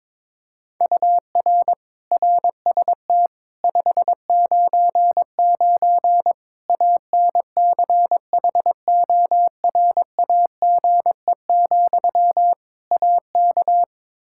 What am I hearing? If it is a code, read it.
Morse code, heard as a keyed tone, about 22 wpm, 700 Hz. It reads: UR RST 599 ANCHORAGE, AK
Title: K1USN SST QSO Practice 22wpm